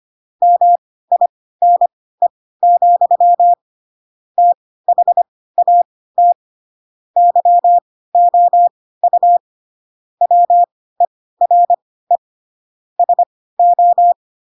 Morse code, heard as a keyed tone, about 25 wpm, spaced as for 16 wpm, 700 Hz. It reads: MINE, THAT YOU WERE SO
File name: Pinocchio / Pinocchio ch7 10 25 ik1hge